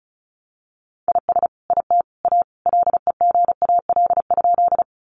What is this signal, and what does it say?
Morse code, heard as a keyed tone, about 34 wpm, 700 Hz. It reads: IS IT A LEGAL?
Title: Sentences from Top 1200 Words 35wpm